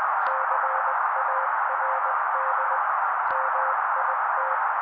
Morse code, heard as a keyed tone, about 20 wpm, 530 Hz. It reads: CARD MIT
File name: daily / 18wpm noise